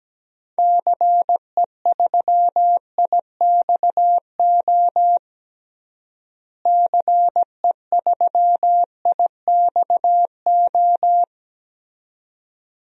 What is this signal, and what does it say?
Morse code, heard as a keyed tone, about 17 wpm, 700 Hz. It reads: CE3IXO CE3IXO
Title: DXCC-100 Call Signs - 6 Characters - Repeated 2x - 17wpm